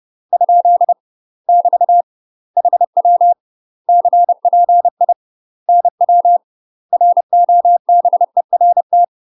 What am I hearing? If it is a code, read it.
Morse code, heard as a keyed tone, about 30 wpm, 700 Hz. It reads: ? <BT> HW CPI NW ROBERT
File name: UK MASTER QSOs TRAINING/30WPM